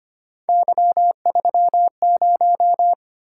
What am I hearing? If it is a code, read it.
Morse code, heard as a keyed tone, about 25 wpm, 700 Hz. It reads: Y30
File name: Three Letter-Number Combinations - 25wpm